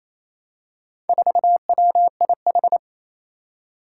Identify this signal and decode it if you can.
Morse code, heard as a keyed tone, about 27 wpm, 700 Hz. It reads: 4WIH
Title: Four Challenging Letter-Number Combinations 28wpm